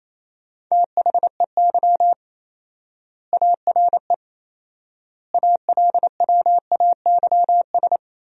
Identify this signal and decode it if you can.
Morse code, heard as a keyed tone, about 27 wpm, 700 Hz. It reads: THEY ARE ALWAYS
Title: 4x Word-Spacing - Sentences from Top 300 Words - 28wpm